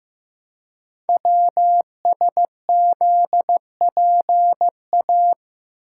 Morse code, heard as a keyed tone, about 15 wpm, 700 Hz. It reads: WSZPA
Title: Code Groups - 5 Letters - 15wpm